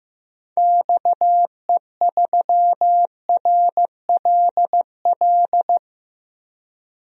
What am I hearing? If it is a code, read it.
Morse code, heard as a keyed tone, about 15 wpm, 700 Hz. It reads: XE3RLL
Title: DXCC-100 Call Signs - 6 Characters - Repeated 3x - 15wpm